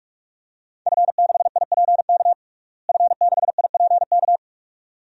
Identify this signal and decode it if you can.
Morse code, heard as a keyed tone, about 46 wpm, 700 Hz. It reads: F6IPK F6IPK
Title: DXCC-100 Call Signs - 5 Characters - Repeated 2x - 45wpm